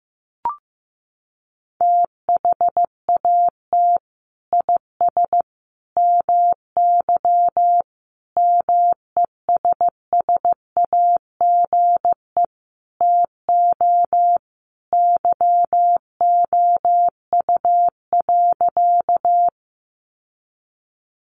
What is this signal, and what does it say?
Morse code, heard as a keyed tone, about 15 wpm, 700 Hz. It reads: THAT IS MY MESSAGE TO YOU.
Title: Sentences from Top 1100 Words 15wpm